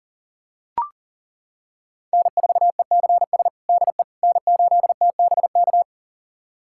Morse code, heard as a keyed tone, about 40 wpm, 700 Hz. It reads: N4ECS DE N8TBK
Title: USA Callsign DE Callsign - 5 Characters - 40wpm